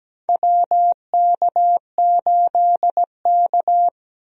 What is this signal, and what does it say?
Morse code, heard as a keyed tone, about 17 wpm, 700 Hz. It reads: WK8K
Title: USA Call Signs - 4 Characters - Repeated 2x - 17wpm